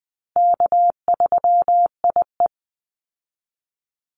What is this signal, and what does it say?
Morse code, heard as a keyed tone, about 20 wpm, 700 Hz. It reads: K3IE
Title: Easy Call Signs Repeated 3x - 20wpm